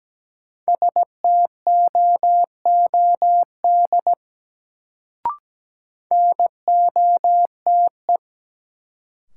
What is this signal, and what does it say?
Morse code, heard as a keyed tone, about 17 wpm, 700 Hz. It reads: STOOD NOTE
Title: Sentences from Top 500 Words - New Word Drill - 17wpm